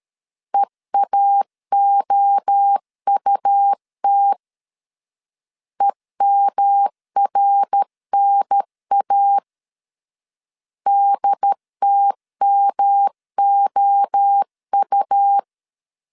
Morse code, heard as a keyed tone, about 13 wpm, 800 Hz. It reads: EAOUT EMRNA DTMOU
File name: CW / DET / 1 3 1500C S A 12wpm 2905